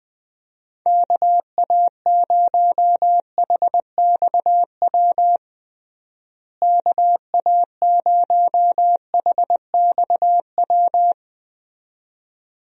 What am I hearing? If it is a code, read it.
Morse code, heard as a keyed tone, about 21 wpm, 700 Hz. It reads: KA0HXW KA0HXW
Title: USA Call Signs - 6 Characters - Repeated 2x - 20wpm